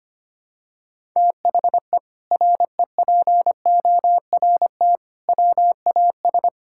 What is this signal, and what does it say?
Morse code, heard as a keyed tone, about 25 wpm, 700 Hz. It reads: THE REPORT WAS
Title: Sentences from Top 1000 Words - Review Part 6 - 25wpm